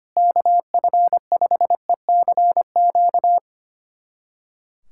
Morse code, heard as a keyed tone, about 25 wpm, 700 Hz. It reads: KF5ECQ
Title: USA Callsign DE Callsign - 6 Characters - 25wpm